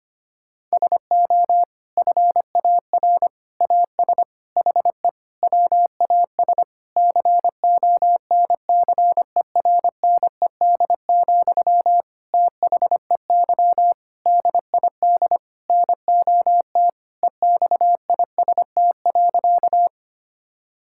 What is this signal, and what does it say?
Morse code, heard as a keyed tone, about 25 wpm, 700 Hz. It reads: SO FAR AS HE WAS CONCERNED, THEY DID NOT EXIST.
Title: Sentences from Top 1000 Words - Review Part 6 - 25wpm